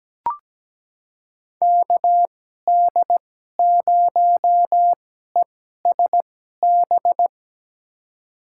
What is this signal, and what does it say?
Morse code, heard as a keyed tone, about 17 wpm, spaced as for 12 wpm, 700 Hz. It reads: KD0ESB
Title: USA Call Signs - 6 Characters - 2x Character Spacing - 17wpm